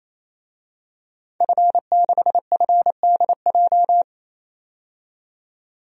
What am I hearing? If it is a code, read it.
Morse code, heard as a keyed tone, about 28 wpm, 700 Hz. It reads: F6FDJ
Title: DXCC-100 Call Signs - 5 Characters - 28wpm